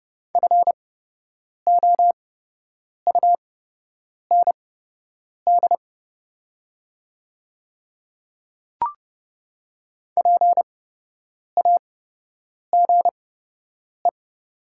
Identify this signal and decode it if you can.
Morse code, heard as a keyed tone, about 30 wpm, spaced as for 8 wpm, 700 Hz. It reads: FOUND PAGE
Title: 8x Character Spacing - Top 300 Words - 30wpm